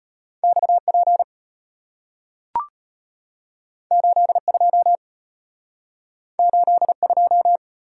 Morse code, heard as a keyed tone, about 34 wpm, 690 Hz. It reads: XP 82 82